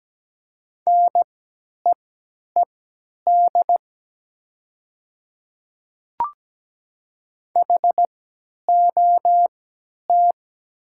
Morse code, heard as a keyed tone, about 17 wpm, spaced as for 10 wpm, 700 Hz. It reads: NEED HOT
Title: 3x Character Spacing - Top 200 Words - 17wpm